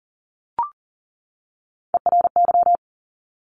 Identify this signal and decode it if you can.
Morse code, heard as a keyed tone, about 40 wpm, 700 Hz. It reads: ERY
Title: Top Three Letter Combinations in Words 40wpm